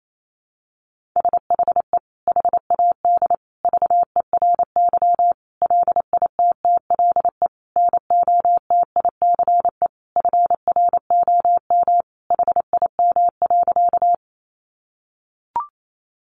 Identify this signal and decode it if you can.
Morse code, heard as a keyed tone, about 28 wpm, 700 Hz. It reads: SHE HAD VERY LITTLE NOTICE FROM HIM.
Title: Sentences from Top 400 Words 28wpm